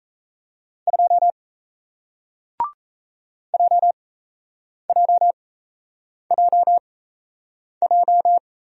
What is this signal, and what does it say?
Morse code, heard as a keyed tone, about 38 wpm, spaced as for 8 wpm, 700 Hz. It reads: 2 JJJJ